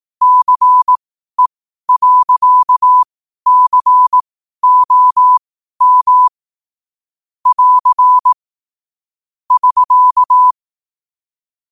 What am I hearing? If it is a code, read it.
Morse code, heard as a keyed tone, about 18 wpm, spaced as for 12 wpm, 1000 Hz. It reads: CE.COM <AR> <SK>
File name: qotdfiles / 12wpm